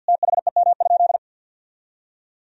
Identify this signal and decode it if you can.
Morse code, heard as a keyed tone, about 50 wpm, 700 Hz. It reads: THEM?
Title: Sentences from Top 1000 Words - Review Part 3 - 50wpm